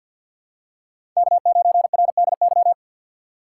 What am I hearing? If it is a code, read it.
Morse code, heard as a keyed tone, about 50 wpm, 700 Hz. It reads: K9RDY
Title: USA Call Signs - 6x Character Spacing - 50wpm